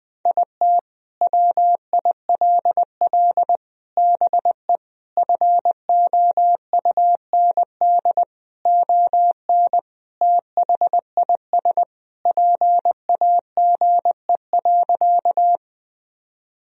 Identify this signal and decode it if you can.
Morse code, heard as a keyed tone, about 21 wpm, 700 Hz. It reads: IT WILL BE FOUND ON THIS PAGE.
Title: Sentences from Top 1000 Words - Review Part 1 - 20wpm